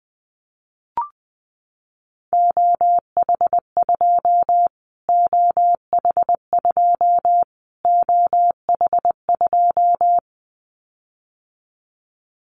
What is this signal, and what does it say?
Morse code, heard as a keyed tone, about 20 wpm, 700 Hz. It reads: OH2 OH2 OH2
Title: Callsign Prefixes for Top 25 DXCC - Repeated 3x - 20wpm